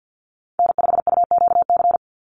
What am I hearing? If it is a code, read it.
Morse code, heard as a keyed tone, about 50 wpm, 700 Hz. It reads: N5UQC
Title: USA Call Signs 50wpm